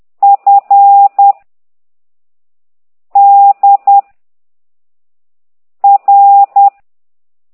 Morse code, heard as a keyed tone, about 10 wpm, 800 Hz. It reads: F D R
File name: ct2idw / exerc9